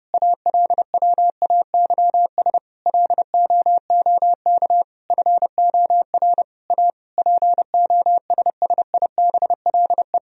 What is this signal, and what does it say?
Morse code, heard as a keyed tone, about 31 wpm, 700 Hz. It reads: ALWAYS LOOK FOR A POSSIBLE
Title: Sentences from Top 1000 Words - Review Part 4 - 30wpm